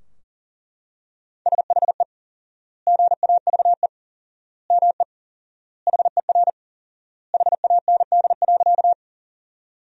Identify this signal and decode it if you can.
Morse code, heard as a keyed tone, about 40 wpm, 700 Hz. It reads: SHE GAVE ME HER HAND.
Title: 4x Word-Spacing - Sentences from Top 500 Words - 40wpm